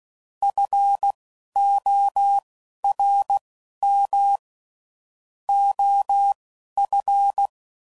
Morse code, heard as a keyed tone, about 16 wpm, spaced as for 12 wpm, 800 Hz. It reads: FORM OF
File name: CW Code Groups @10WPM / Technical Subjects / SwPwrSup